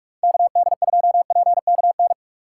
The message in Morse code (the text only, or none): KD2PKN